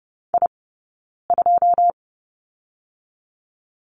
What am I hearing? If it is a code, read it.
Morse code, heard as a keyed tone, about 30 wpm, 700 Hz. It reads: I 2
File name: course / 30:10wpm / 086 Lesson 15d - Number 5 - Two Characters 30wpm